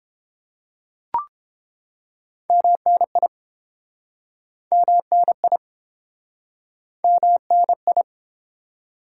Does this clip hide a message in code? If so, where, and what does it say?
Yes: Morse code from 2.5 to 8.0 s, heard as a keyed tone, about 32 wpm, 700 Hz. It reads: MNI MNI MNI